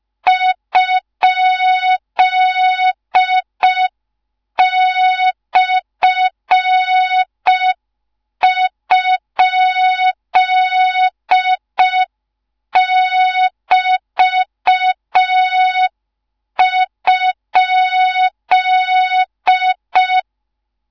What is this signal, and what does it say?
Morse code, heard as a keyed tone, about 5 wpm, 3700 Hz. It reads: ?/?<BT>?